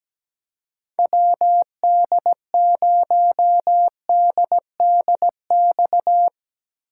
Morse code, heard as a keyed tone, about 17 wpm, 700 Hz. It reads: WD0DDX
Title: USA Call Signs - 3x Character Spacing - 17wpm